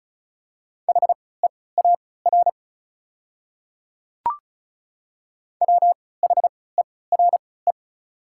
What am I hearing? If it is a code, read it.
Morse code, heard as a keyed tone, about 36 wpm, spaced as for 20 wpm, 690 Hz. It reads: HEAR WHERE
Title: 3x Character Spacing - Top 300 Words - 35wpm